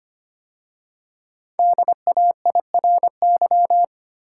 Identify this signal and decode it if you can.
Morse code, heard as a keyed tone, about 25 wpm, 700 Hz. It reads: DAIRY
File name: 5 Letter Words - First 3 Letters Given - 25wpm